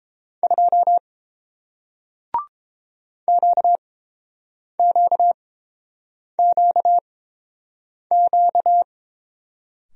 Morse code, heard as a keyed tone, about 31 wpm, spaced as for 7 wpm, 700 Hz. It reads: 2 QQQQ